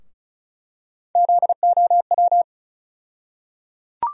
Morse code, heard as a keyed tone, about 36 wpm, 700 Hz. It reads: ZOW
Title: USA Call Sign Postfixes - 3 Letters 35wpm